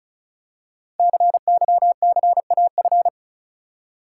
Morse code, heard as a keyed tone, about 36 wpm, 690 Hz. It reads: CYCAF
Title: Code Groups - 5 Letters - 35wpm